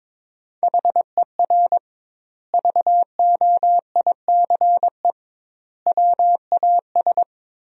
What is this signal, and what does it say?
Morse code, heard as a keyed tone, about 22 wpm, 700 Hz. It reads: HER VOICE WAS
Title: 2x Word-Spacing - Sentences from Top 500 Words - 22wpm